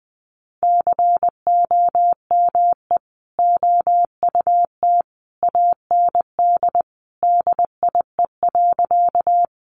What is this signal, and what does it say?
Morse code, heard as a keyed tone, about 20 wpm, 700 Hz. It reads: COME OUT AND DIE.
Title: Sentences from Top 1300 Words 20wpm